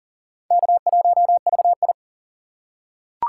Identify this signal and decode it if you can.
Morse code, heard as a keyed tone, about 40 wpm, 700 Hz. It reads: K1VI
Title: USA Call Signs - 3x Character Spacing - 40wpm